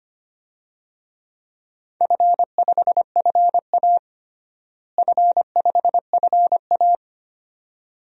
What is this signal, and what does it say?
Morse code, heard as a keyed tone, about 25 wpm, 700 Hz. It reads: F5FA F5FA
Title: DXCC-100 Call Signs - 4 Characters - Repeated 2x - 25wpm